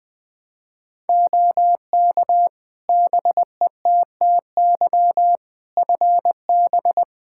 Morse code, heard as a keyed tone, about 20 wpm, 700 Hz. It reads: OK BETTY FB
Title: Practice QSO - Exchange 2 - Info - 20wpm